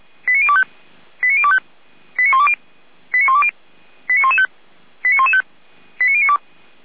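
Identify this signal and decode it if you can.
Five-tone selective call (ZVEI), heard as quick runs of five tones, tones of about 70 ms. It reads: 89025 89025 89120 89120 89115 89115 89092
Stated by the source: Five-tone ZVEI selective call received by SDR.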